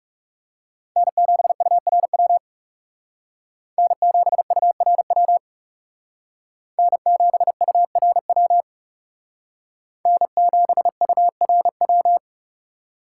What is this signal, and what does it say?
Morse code, heard as a keyed tone, about 36 wpm, 700 Hz. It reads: N7URW N7URW N7URW N7URW
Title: Speed Racing - USA 1x3 Call Signs 30wpm